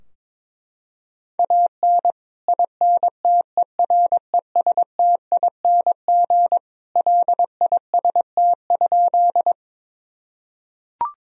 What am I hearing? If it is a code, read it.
Morse code, heard as a keyed tone, about 22 wpm, 700 Hz. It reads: AN INTERESTING LIST?
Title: Sentences from Top 1000 Words - Review Part 2 - 22wpm